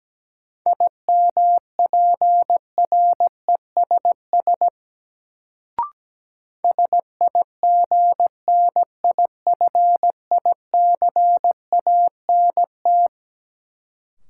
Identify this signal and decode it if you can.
Morse code, heard as a keyed tone, about 17 wpm, 700 Hz. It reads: IMPRESS SIGNIFICANT
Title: Sentences from Top 1600 Words - New Word Drill Alt - 17wpm